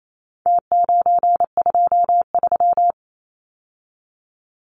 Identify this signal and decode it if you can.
Morse code, heard as a keyed tone, about 28 wpm, 700 Hz. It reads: T923
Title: Numbers - Contest Style - 4-Digit 28wpm